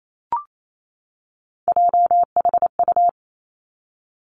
Morse code, heard as a keyed tone, about 28 wpm, 700 Hz. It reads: JHU
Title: Three Challenging Letter-Number Combinations 28wpm